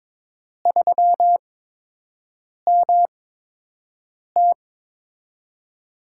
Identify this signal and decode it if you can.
Morse code, heard as a keyed tone, about 22 wpm, spaced as for 6 wpm, 700 Hz. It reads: 3MT
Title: USA Call Signs - 8x Character Spacing - 22wpm